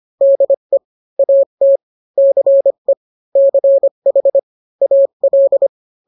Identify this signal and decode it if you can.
Morse code, heard as a keyed tone, about 25 wpm, 550 Hz. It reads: DE AT CE CH AL